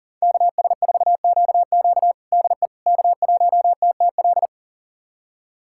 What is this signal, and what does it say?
Morse code, heard as a keyed tone, about 37 wpm, 700 Hz. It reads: KS4QQ DE K1TTL
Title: USA Callsign DE Callsign - 5 Characters - 40wpm